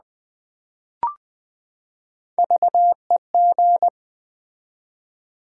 Morse code, heard as a keyed tone, about 20 wpm, 700 Hz. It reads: VEG
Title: Build A Word - 3 Letters At A Time - 9-Letter Words - 20wpm